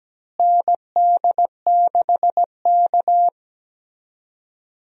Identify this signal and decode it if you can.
Morse code, heard as a keyed tone, about 17 wpm, 700 Hz. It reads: ND6K
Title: USA Call Signs - 4 Characters - Repeated 3x - 17wpm